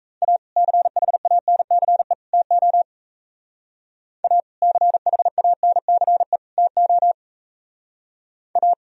Morse code, heard as a keyed tone, about 35 wpm, 700 Hz. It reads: A CHANCE TO A CHANCE TO A